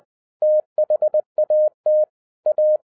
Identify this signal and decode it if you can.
Morse code, heard as a keyed tone, about 20 wpm, 610 Hz. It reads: THAT A
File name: 2012-01-mp3 / n16-THROUGH EIGHT MONTHS-20wpm